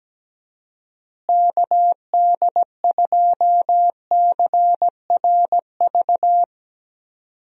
Morse code, heard as a keyed tone, about 17 wpm, 700 Hz. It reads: KD2CRV
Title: USA Call Signs - 6 Characters - Repeated 2x - 17wpm